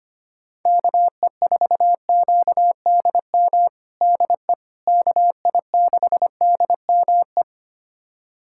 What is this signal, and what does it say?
Morse code, heard as a keyed tone, about 25 wpm, 700 Hz. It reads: KE4QDM DE KI6DME
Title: USA Callsign DE Callsign - 6 Characters - 25wpm